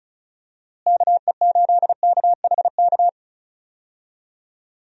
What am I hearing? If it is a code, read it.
Morse code, heard as a keyed tone, about 36 wpm, 700 Hz. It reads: KE8KHK
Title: USA Call Signs - 6 Characters - Repeated 3x - 35wpm